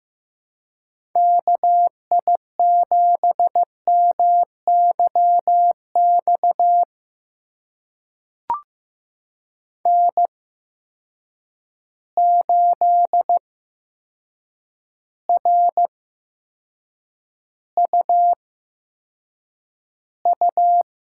Morse code, heard as a keyed tone, about 15 wpm, spaced as for 7 wpm, 700 Hz. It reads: KI7MYX N8RUU
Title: USA Call Signs - 8x Character Spacing - 15wpm